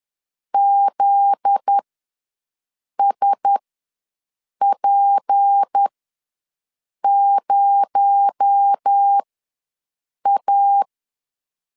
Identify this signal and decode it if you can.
Morse code, heard as a keyed tone, about 10 wpm, spaced as for 6 wpm, 800 Hz. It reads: ZSP0A
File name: LEZIONI / 9 28 ws 10wpm 4124